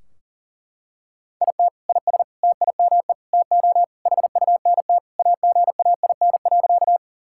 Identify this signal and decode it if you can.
Morse code, heard as a keyed tone, about 40 wpm, 700 Hz. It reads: IT IS TIME TO HUNT AGAIN.
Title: Sentences from Top 1600 Words 40wpm